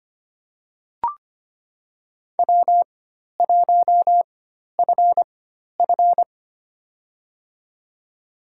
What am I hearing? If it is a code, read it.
Morse code, heard as a keyed tone, about 25 wpm, spaced as for 12 wpm, 700 Hz. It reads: W1FF
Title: USA Call Signs - 4x Character Spacing - 25wpm